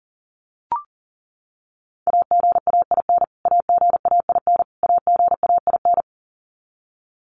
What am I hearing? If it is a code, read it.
Morse code, heard as a keyed tone, about 40 wpm, 700 Hz. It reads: AGAIN AGAIN AGAIN
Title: Top 300 Words Repeated 40wpm